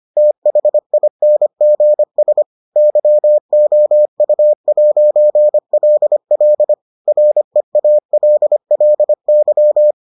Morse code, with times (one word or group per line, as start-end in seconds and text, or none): 0.2-2.4 THINGS
2.8-6.7 YOU'LL
7.1-9.9 REALLY